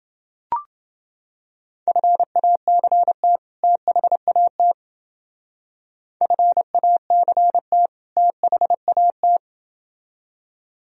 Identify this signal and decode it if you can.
Morse code, heard as a keyed tone, about 28 wpm, 700 Hz. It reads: FACT THAT FACT THAT